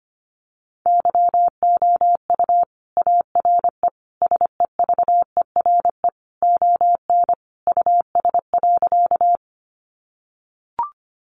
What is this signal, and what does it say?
Morse code, heard as a keyed tone, about 25 wpm, 700 Hz. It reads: YOU ARE SEVERE ON US.
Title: Sentences from Top 1400 Words 25wpm